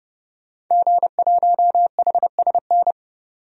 Morse code, heard as a keyed tone, about 30 wpm, 700 Hz. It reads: G1HSN
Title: DXCC-100 Call Signs - 5 Characters - 30wpm